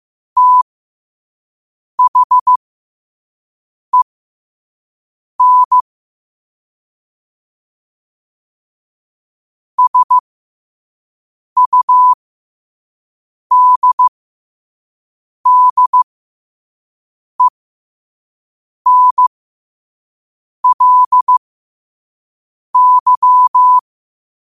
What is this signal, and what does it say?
Morse code, heard as a keyed tone, about 15 wpm, spaced as for 5 wpm, 1000 Hz. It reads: THEN SUDDENLY
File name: qotdfiles / 5wpm